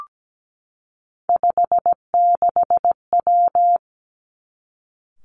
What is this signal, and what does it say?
Morse code, heard as a keyed tone, about 17 wpm, 700 Hz. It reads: 56W
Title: Three Challenging Letter-Number Combinations 17wpm